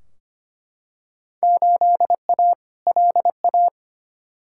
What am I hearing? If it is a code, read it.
Morse code, heard as a keyed tone, about 25 wpm, 700 Hz. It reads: 8A LA
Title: Field Day Exchanges Encore - 25wpm